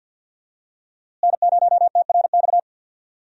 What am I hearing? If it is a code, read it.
Morse code, heard as a keyed tone, about 52 wpm, 700 Hz. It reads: N0TRX
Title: USA Call Signs - 5 Characters - Repeated 2x - 50wpm